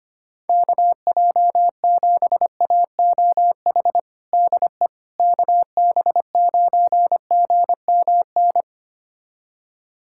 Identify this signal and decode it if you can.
Morse code, heard as a keyed tone, about 25 wpm, 700 Hz. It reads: KJ7AOH DE KB9GMN
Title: USA Callsign DE Callsign - 6 Characters - 25wpm